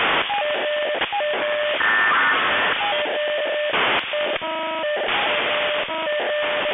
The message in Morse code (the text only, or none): none